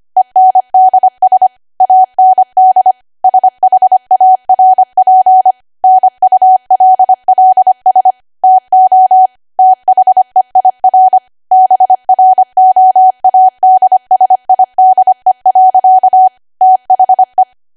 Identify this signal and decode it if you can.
Morse code, heard as a keyed tone, about 25 wpm, 750 Hz. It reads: ENDS AND SHARP NULLS TO THEIR BROADSIDE. THE
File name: Archive / 25 WPM / 25WPM